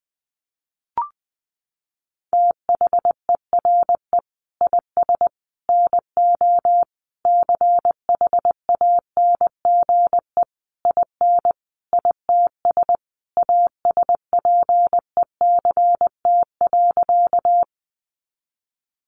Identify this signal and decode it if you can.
Morse code, heard as a keyed tone, about 20 wpm, 700 Hz. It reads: THERE IS NO CHANGE IN ITS ASPECT.
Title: Sentences from Top 1300 Words 20wpm